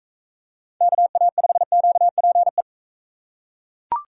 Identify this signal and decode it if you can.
Morse code, heard as a keyed tone, about 40 wpm, 700 Hz. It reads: KA5QPE